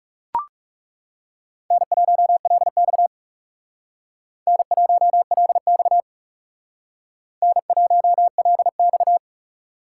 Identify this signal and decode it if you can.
Morse code, heard as a keyed tone, about 39 wpm, 700 Hz. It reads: N1LX N1LX N1LX